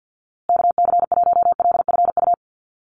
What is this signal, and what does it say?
Morse code, heard as a keyed tone, about 48 wpm, 700 Hz. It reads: KC1LFU
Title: USA Call Signs Repeated 2x 50wpm